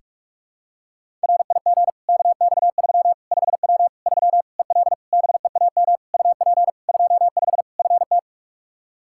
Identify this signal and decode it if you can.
Morse code, heard as a keyed tone, about 46 wpm, 700 Hz. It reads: RIG KX3 5W 3 EL BEAM UP 25 FT
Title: K1USN SST QSO Practice 45wpm